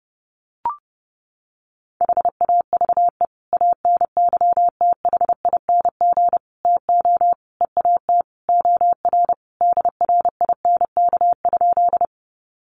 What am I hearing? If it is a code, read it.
Morse code, heard as a keyed tone, about 28 wpm, 700 Hz. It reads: HAVE ANYTHING TO EAT OR DRINK?
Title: Sentences from Top 1500 Words 30wpm